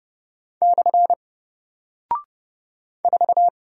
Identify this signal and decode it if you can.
Morse code, heard as a keyed tone, about 30 wpm, 700 Hz. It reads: / 4
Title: Single Character (Letter:Number:Punctuation:Prosign) 30wpm